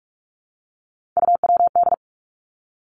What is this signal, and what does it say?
Morse code, heard as a keyed tone, about 46 wpm, 700 Hz. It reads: UWD
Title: Three Challenging Letter-Number Combinations 45wpm